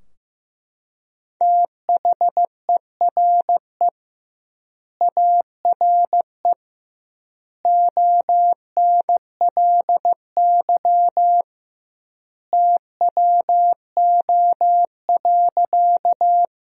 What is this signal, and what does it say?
Morse code, heard as a keyed tone, about 15 wpm, 700 Hz. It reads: THERE ARE ONLY TWO.
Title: 2x Word-Spacing - Sentences from Top 200 Words - 15wpm